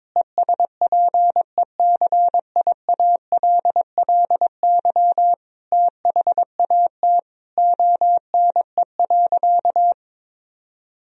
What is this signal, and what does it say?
Morse code, heard as a keyed tone, about 22 wpm, 700 Hz. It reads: ESPECIALLY THAT ONE.
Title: Sentences from Top 1000 Words - Review Part 4 - 22wpm